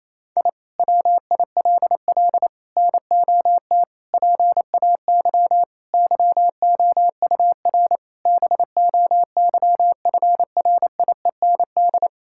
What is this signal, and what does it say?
Morse code, heard as a keyed tone, about 28 wpm, 700 Hz. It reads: I WILL NOT PAY YOUR BOYFRIEND
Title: Sentences from Top 1000 Words - Review Part 5 - 28wpm